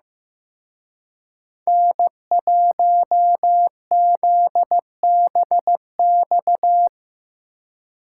Morse code, heard as a keyed tone, about 18 wpm, 700 Hz. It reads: N1ZBX
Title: Speed Racing - USA 1x3 Call Signs 15wpm